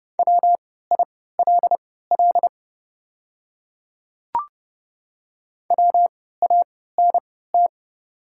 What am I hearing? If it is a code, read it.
Morse code, heard as a keyed tone, about 30 wpm, spaced as for 17 wpm, 700 Hz. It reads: WILL WANT
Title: 3x Character Spacing - Top 300 Words - 30wpm